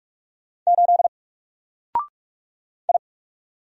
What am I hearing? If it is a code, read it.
Morse code, heard as a keyed tone, about 45 wpm, 700 Hz. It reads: 8 I
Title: Single Letter-Number Alt 45wpm